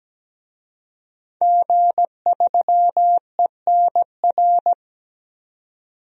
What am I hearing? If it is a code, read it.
Morse code, heard as a keyed tone, about 17 wpm, 700 Hz. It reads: G3ENR
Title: DXCC-100 Call Signs - 5 Characters - 17wpm